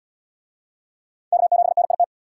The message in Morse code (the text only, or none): K7NIT